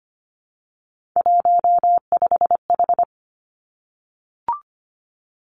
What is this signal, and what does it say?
Morse code, heard as a keyed tone, about 25 wpm, 700 Hz. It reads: 15H
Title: Three Challenging Letter-Number Combinations 25wpm